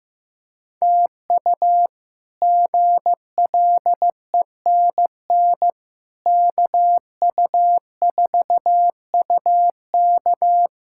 Morse code, heard as a keyed tone, about 15 wpm, 700 Hz. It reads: TU GLENN KU4UK
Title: K1USN SST QSO Practice - 15wpm